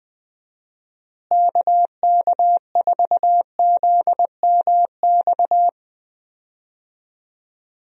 Morse code, heard as a keyed tone, about 21 wpm, 700 Hz. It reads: KK4ZMX
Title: USA Call Signs - 6 Characters - Repeated 2x - 20wpm